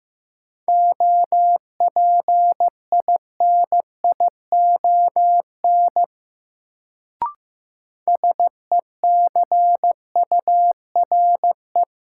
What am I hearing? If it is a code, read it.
Morse code, heard as a keyed tone, about 15 wpm, 700 Hz. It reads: OPINION SECURE
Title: Sentences from Top 1200 Words - New Word Drill - 15wpm